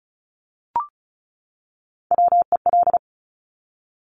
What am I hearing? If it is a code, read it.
Morse code, heard as a keyed tone, about 36 wpm, 700 Hz. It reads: WEL
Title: Top Three Letter Combinations in Words 35wpm